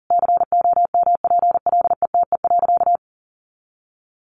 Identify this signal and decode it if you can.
Morse code, heard as a keyed tone, about 40 wpm, 700 Hz. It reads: COMPLETE.
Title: Sentences from Top 400 Words 40wpm